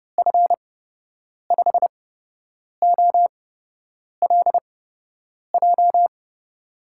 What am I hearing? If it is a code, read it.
Morse code, heard as a keyed tone, about 32 wpm, spaced as for 8 wpm, 700 Hz. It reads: F5OLJ
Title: USA Call Signs - 6 Characters - 8x Character Spacing - 30wpm